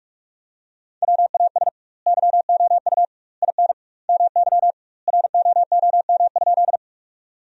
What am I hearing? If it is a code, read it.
Morse code, heard as a keyed tone, about 46 wpm, 700 Hz. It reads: WAS YOU IN MY ROOM?
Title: 2x Word-Spacing - Sentences from Top 300 Words - 45wpm